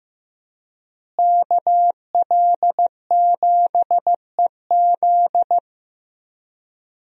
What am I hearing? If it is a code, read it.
Morse code, heard as a keyed tone, about 15 wpm, 700 Hz. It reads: KL7EZ
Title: DXCC-100 Call Signs - 5 Characters - Repeated 3x - 15wpm